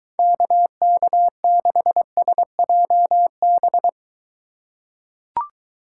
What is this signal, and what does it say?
Morse code, heard as a keyed tone, about 23 wpm, 700 Hz. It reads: KK6SJB